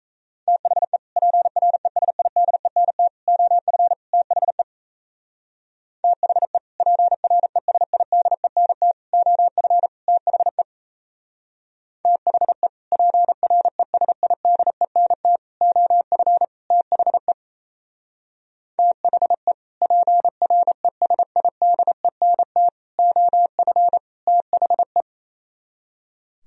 Morse code, heard as a keyed tone, about 35 wpm, 700 Hz. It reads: THE PRESIDENT OF THE THE PRESIDENT OF THE THE PRESIDENT OF THE THE PRESIDENT OF THE